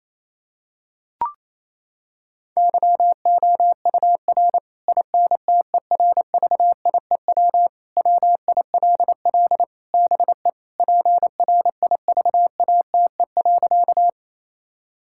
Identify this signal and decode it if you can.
Morse code, heard as a keyed tone, about 28 wpm, 700 Hz. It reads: YOUR INTERVIEW WILL BE PRIVATE.
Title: Sentences from Top 1600 Words 28wpm